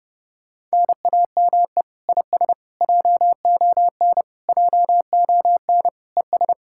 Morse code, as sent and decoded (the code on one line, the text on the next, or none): -. .- -- . / .. ... / .--- --- -. / .--- --- -. / . ...
NAME IS JON JON ES